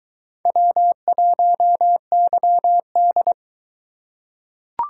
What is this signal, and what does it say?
Morse code, heard as a keyed tone, about 22 wpm, 700 Hz. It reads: W1YD